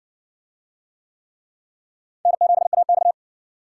Morse code, heard as a keyed tone, about 60 wpm, 700 Hz. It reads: N7AX